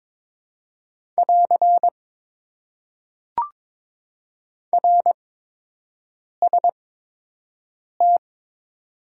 Morse code, heard as a keyed tone, about 22 wpm, spaced as for 6 wpm, 700 Hz. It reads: <AR> RST
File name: Top 90 QSO Elements - 8x Character Spacing - 22wpm